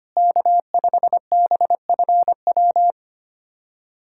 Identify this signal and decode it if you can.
Morse code, heard as a keyed tone, about 25 wpm, 700 Hz. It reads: K5BFW
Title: USA Callsign DE Callsign 25wpm